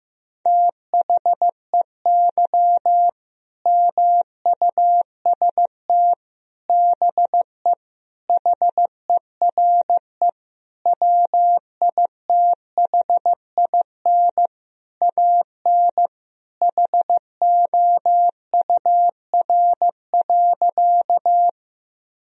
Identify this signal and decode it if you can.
Morse code, heard as a keyed tone, about 15 wpm, 700 Hz. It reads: THEY MUST BE HERE WITHIN AN HOUR.
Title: Sentences from Top 1000 Words - Review Part 4 - 15wpm